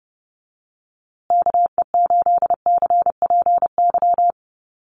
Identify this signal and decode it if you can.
Morse code, heard as a keyed tone, about 30 wpm, 700 Hz. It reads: KE8CPY
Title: USA Call Signs Repeated 3x 30wpm